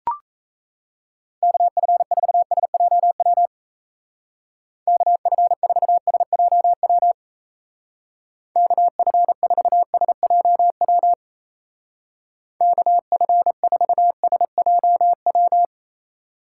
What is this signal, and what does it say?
Morse code, heard as a keyed tone, about 34 wpm, 700 Hz. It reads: KF4SJW KF4SJW KF4SJW KF4SJW
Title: Speed Racing - USA Call Signs 28wpm